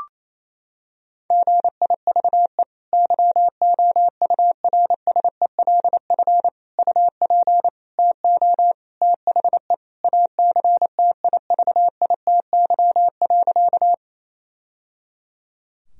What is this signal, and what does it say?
Morse code, heard as a keyed tone, about 28 wpm, 700 Hz. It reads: GIVE YOURSELF UP TO THE ACTIVITY.
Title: Sentences from Top 1000 Words - Review Part 4 - 28wpm